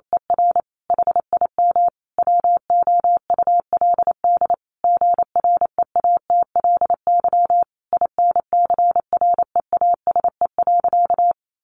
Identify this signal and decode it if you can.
Morse code, heard as a keyed tone, about 28 wpm, 700 Hz. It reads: ER HIM WOULD GREATLY INCREASE.
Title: Sentences from Top 1400 Words 28wpm